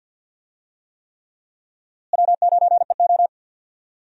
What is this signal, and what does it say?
Morse code, heard as a keyed tone, about 51 wpm, 700 Hz. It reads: W9EO
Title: USA Call Signs - 4 Characters - Repeated 3x - 50wpm